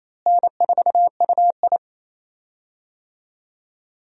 Morse code, heard as a keyed tone, about 28 wpm, 700 Hz. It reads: N4UI
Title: USA Call Signs - 4 Characters - Repeated 2x - 28wpm